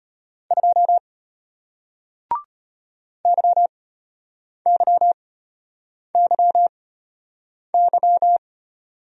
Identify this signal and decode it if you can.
Morse code, heard as a keyed tone, about 33 wpm, spaced as for 8 wpm, 700 Hz. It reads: 2 YYYY